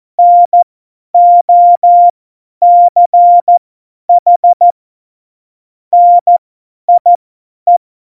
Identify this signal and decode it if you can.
Morse code, heard as a keyed tone, about 14 wpm, spaced as for 10 wpm, 700 Hz. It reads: NOCH NIE
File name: Morsekurs / Witze / Witz 15. August 17, Tempo 50 F70